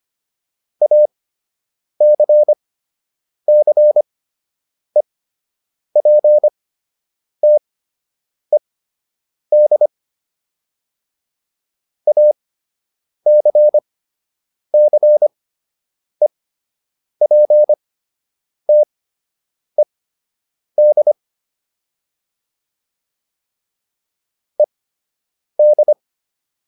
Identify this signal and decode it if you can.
Morse code, heard as a keyed tone, about 25 wpm, spaced as for 8 wpm, 600 Hz. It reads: ACCEPTED ACCEPTED ED